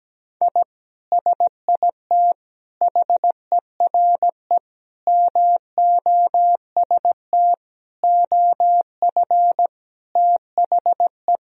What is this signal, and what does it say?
Morse code, heard as a keyed tone, about 17 wpm, 700 Hz. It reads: I SIT HERE MOST OF THE
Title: Sentences from Top 1000 Words - Review Part 1 - 17wpm